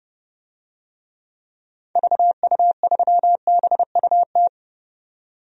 Morse code, heard as a keyed tone, about 30 wpm, 700 Hz. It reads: VU3BUT
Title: DXCC-100 Call Signs - 6 Characters - 30wpm